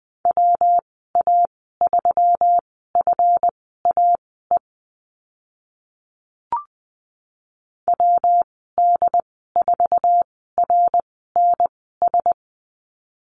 Morse code, heard as a keyed tone, about 20 wpm, spaced as for 14 wpm, 700 Hz. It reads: WA3FAE WD4RNS
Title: 140 Lesson 23g - Letter f - Call Signs 20wpm